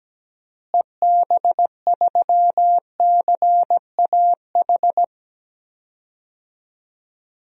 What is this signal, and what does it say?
Morse code, heard as a keyed tone, about 17 wpm, 700 Hz. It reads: EB3CAH
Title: DXCC-100 Call Signs - 6 Characters - Repeated 2x - 17wpm